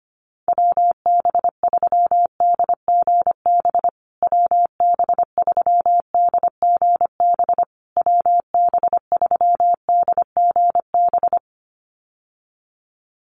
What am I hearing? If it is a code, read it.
Morse code, heard as a keyed tone, about 25 wpm, 700 Hz. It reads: WB3DGB WB3DGB WB3DGB
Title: USA Call Signs Repeated 3x 25wpm